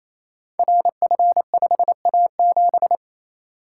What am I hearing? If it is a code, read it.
Morse code, heard as a keyed tone, about 29 wpm, 700 Hz. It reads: RF5A7
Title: Code Groups - 5 Letter:Numbers - 28wpm